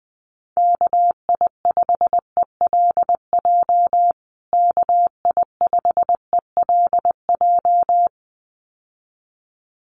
Morse code, heard as a keyed tone, about 20 wpm, 700 Hz. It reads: KI5ELJ KI5ELJ
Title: USA Call Signs Repeated 2x 20wpm